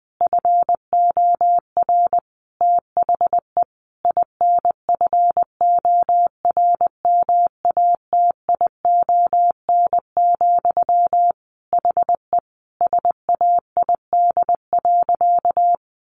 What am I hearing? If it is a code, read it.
Morse code, heard as a keyed tone, about 19 wpm, 700 Hz. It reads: FOR THE INFORMATION, HE SAID.
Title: Sentences from Top 600 Words 20wpm